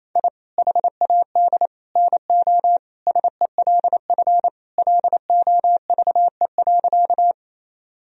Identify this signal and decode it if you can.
Morse code, heard as a keyed tone, about 28 wpm, 700 Hz. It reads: I HAD NO SELF LOVE.
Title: Sentences from Top 1000 Words - Review Part 1 - 28wpm